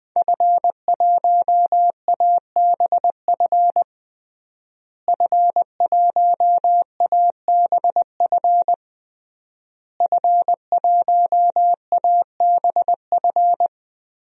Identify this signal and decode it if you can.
Morse code, heard as a keyed tone, about 20 wpm, 700 Hz. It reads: F1ABF F1ABF F1ABF
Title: DXCC-100 Call Signs - 5 Characters - Repeated 3x - 20wpm